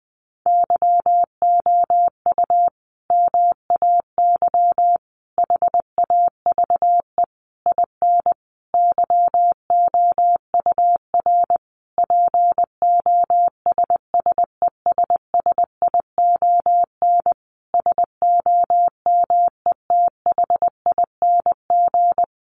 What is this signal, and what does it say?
Morse code, heard as a keyed tone, about 20 wpm, 700 Hz. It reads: YOU MAY HAVE IN YOUR POSSESSION SOMETHING
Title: Sentences from Top 1400 Words 20wpm